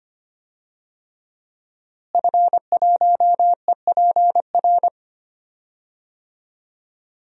Morse code, heard as a keyed tone, about 25 wpm, 700 Hz. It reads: F1EPR
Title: DXCC-100 Call Signs - 5 Characters - Repeated 3x - 25wpm